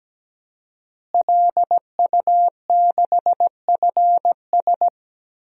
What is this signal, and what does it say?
Morse code, heard as a keyed tone, about 17 wpm, 700 Hz. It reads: LU6FS
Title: DXCC-100 Call Signs - 5 Characters - Repeated 2x - 17wpm